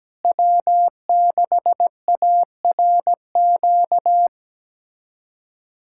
Morse code, heard as a keyed tone, about 17 wpm, 700 Hz. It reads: W6ARQ
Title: USA Callsign DE Callsign - 5 Characters - 17wpm